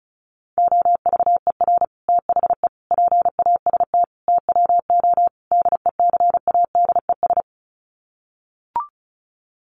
Morse code, heard as a keyed tone, about 34 wpm, 700 Hz. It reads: OVER THE PAST TWO DECADES
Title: Sets of 5 Words Encore2 35wpm